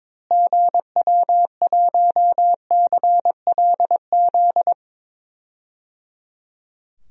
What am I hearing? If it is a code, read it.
Morse code, heard as a keyed tone, about 22 wpm, 700 Hz. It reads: GW1CLZ
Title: DXCC-100 Call Signs - 6 Characters - Repeated 3x - 22wpm